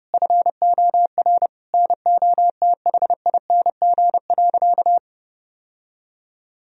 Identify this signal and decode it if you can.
Morse code, heard as a keyed tone, about 32 wpm, 700 Hz. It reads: FOR NOTHING.
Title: Sentences from Top 1000 Words - Review Part 3 - 30wpm